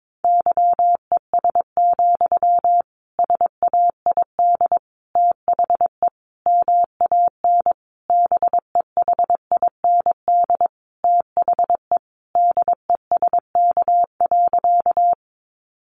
Morse code, heard as a keyed tone, about 22 wpm, 700 Hz. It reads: YES, SAID THE MAN BEHIND THE DESK.
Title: A Sound Of Thunder 22wpm